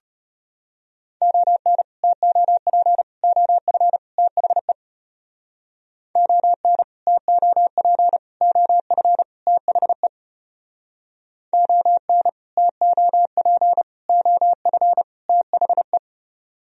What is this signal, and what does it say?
Morse code, heard as a keyed tone, about 32 wpm, 700 Hz. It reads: ON TOP OF THE ON TOP OF THE ON TOP OF THE